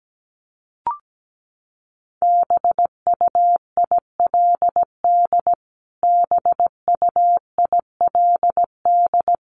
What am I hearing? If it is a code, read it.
Morse code, heard as a keyed tone, about 17 wpm, 700 Hz. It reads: BUILD BUILD
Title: Top 200 Words Repeated 17wpm